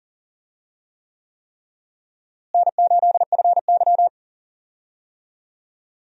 Morse code, heard as a keyed tone, about 38 wpm, 700 Hz. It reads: N8FY
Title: USA Call Signs - 4 Characters - Repeated 3x - 40wpm